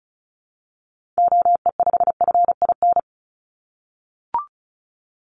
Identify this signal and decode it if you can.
Morse code, heard as a keyed tone, about 36 wpm, 700 Hz. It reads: OE5FIN
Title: Call Signs - 35wpm